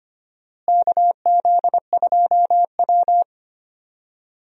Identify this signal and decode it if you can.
Morse code, heard as a keyed tone, about 25 wpm, 700 Hz. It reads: KZ2W
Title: USA Call Signs - 4 Characters - Repeated 3x - 25wpm